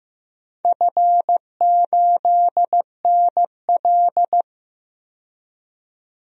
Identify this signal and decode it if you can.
Morse code, heard as a keyed tone, about 15 wpm, 700 Hz. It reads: F8NL
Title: DXCC-100 Call Signs - 4 Characters - 15wpm